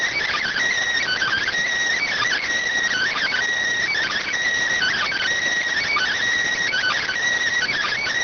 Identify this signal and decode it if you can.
Emergency Alert System (SAME) tones (header bursts), heard as harsh buzzing bursts. It is heard from 1.0 to 7.1 s.